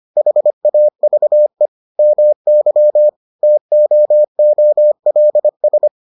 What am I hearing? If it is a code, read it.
Morse code, heard as a keyed tone, about 25 wpm, 600 Hz. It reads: HAVE MY TOOLS